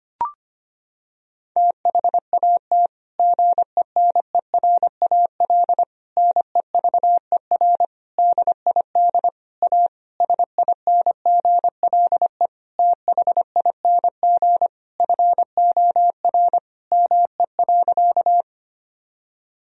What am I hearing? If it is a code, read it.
Morse code, heard as a keyed tone, about 25 wpm, 700 Hz. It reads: THAT GENERAL NEVER DID A SINGLE THING FOR ME.
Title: Sentences from Top 1000 Words - Review Part 4 - 25wpm